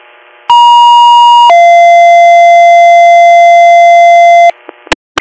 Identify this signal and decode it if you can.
Two-tone page, heard as a one-second tone, then a three-second tone, about 950 then 690 Hz. It is heard from 0.5 to 4.5 s.